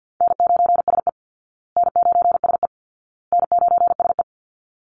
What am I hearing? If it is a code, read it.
Morse code, heard as a keyed tone, about 50 wpm, 700 Hz. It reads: N9SE N9SE N9SE
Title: Easy Call Signs Repeated 3x - 50wpm